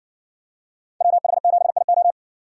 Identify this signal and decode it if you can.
Morse code, heard as a keyed tone, about 60 wpm, 700 Hz. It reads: WH7IO